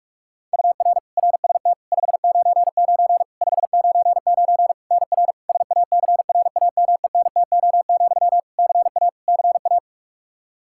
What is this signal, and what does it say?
Morse code, heard as a keyed tone, about 46 wpm, 700 Hz. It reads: UR RST 599 599 NR SACRAMENTO, CA CA
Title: K1USN SST QSO Practice 45wpm